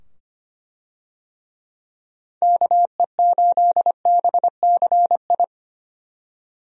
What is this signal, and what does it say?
Morse code, heard as a keyed tone, about 25 wpm, 700 Hz. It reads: KE8BCI
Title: USA Call Signs - 6 Characters - 25wpm